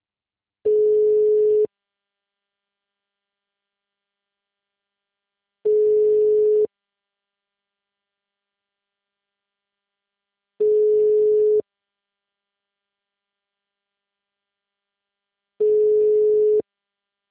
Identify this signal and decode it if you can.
Ringback tone (Europe, 425 Hz).